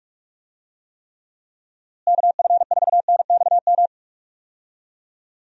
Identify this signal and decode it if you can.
Morse code, heard as a keyed tone, about 46 wpm, 700 Hz. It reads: KF4NXM
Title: USA Call Signs - 6 Characters - Repeated 2x - 45wpm